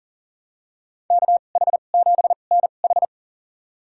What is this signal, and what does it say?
Morse code, heard as a keyed tone, about 37 wpm, spaced as for 28 wpm, 700 Hz. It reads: KH7NH
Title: USA Call Signs - 2x Character Spacing - 40wpm